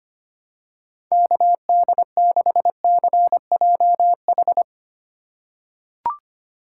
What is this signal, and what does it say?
Morse code, heard as a keyed tone, about 25 wpm, 690 Hz. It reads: KD6CJH
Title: USA Call Signs - 6 Characters - 3x Character Spacing - 25wpm